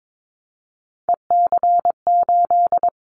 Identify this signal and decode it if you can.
Morse code, heard as a keyed tone, about 22 wpm, 700 Hz. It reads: EC8
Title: Callsign Prefixes for Top 50 DXCC - 22wpm